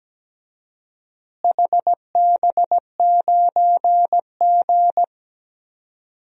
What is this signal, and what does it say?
Morse code, heard as a keyed tone, about 17 wpm, 700 Hz. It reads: HB9G
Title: DXCC-100 Call Signs - 4 Characters - Repeated 3x - 17wpm